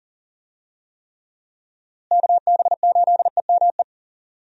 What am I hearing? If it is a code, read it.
Morse code, heard as a keyed tone, about 40 wpm, 700 Hz. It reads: KB8EME
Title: USA Call Signs - 6 Characters - Repeated 3x - 40wpm